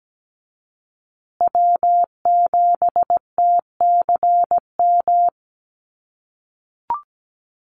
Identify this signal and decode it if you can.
Morse code, heard as a keyed tone, about 17 wpm, 700 Hz. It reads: W7TCM
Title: USA Call Signs Repeated 2x 17wpm